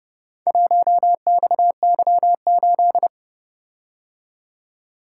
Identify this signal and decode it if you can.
Morse code, heard as a keyed tone, about 29 wpm, 700 Hz. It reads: 1XY8